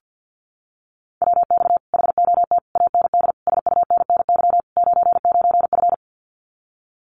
Morse code, heard as a keyed tone, about 50 wpm, 700 Hz. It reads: WX HOT AND SUNNY 99F
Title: Common QSO Phrases Encore 50wpm